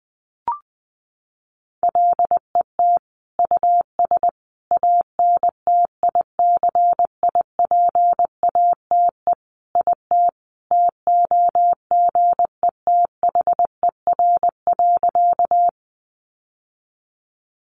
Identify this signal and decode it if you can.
Morse code, heard as a keyed tone, about 19 wpm, 700 Hz. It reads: LET US ANTICIPATE IT TOGETHER.
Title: Sentences from Top 1500 Words 20wpm